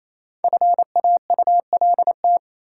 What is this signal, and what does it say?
Morse code, heard as a keyed tone, about 28 wpm, 700 Hz. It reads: FAULT
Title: Five Letter Words - 28wpm